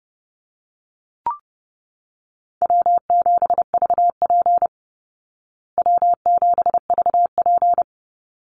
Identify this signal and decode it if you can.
Morse code, heard as a keyed tone, about 28 wpm, 700 Hz. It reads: W7VP W7VP
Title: Call Signs Repeated 3x - 30wpm